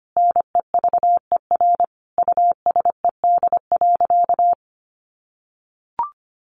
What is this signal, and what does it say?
Morse code, heard as a keyed tone, about 24 wpm, 700 Hz. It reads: NEVER USED.
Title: Sentences from Top 600 Words 25wpm